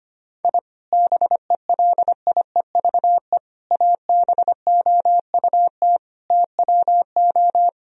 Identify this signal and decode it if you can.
Morse code, heard as a keyed tone, about 25 wpm, 700 Hz. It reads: I BELIEVE ABOUT TWO
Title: Sentences from Top 1000 Words - Review Part 3 - 25wpm